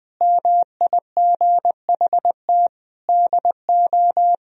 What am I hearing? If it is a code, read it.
Morse code, heard as a keyed tone, about 19 wpm, 700 Hz. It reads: MIGHT DO
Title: Sentences from Top 1000 Words - Review Part 4 - 20wpm